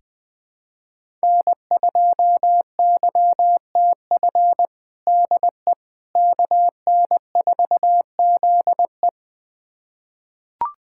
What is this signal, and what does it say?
Morse code, heard as a keyed tone, about 20 wpm, 700 Hz. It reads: N2YTF DE KN4ZE
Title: USA Callsign DE Callsign - 5 Characters - 20wpm